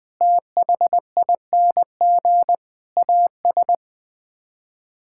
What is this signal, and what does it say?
Morse code, heard as a keyed tone, about 21 wpm, 700 Hz. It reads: THING AS
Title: Sets of 6 Words - 20wpm